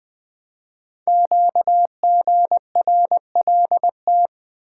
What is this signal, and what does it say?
Morse code, heard as a keyed tone, about 20 wpm, 700 Hz. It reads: QGRLT
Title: Code Groups - 5 Letters - 20wpm